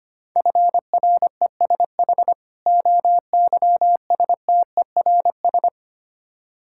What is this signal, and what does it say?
Morse code, heard as a keyed tone, about 25 wpm, 690 Hz. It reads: FRESH OYSTERS
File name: practice / 25wpm / Difficult Words with Hints - 25wpm